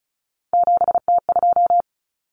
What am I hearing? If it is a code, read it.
Morse code, heard as a keyed tone, about 34 wpm, 700 Hz. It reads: 7T2
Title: Common Cut Numbers - 3-Digit 35wpm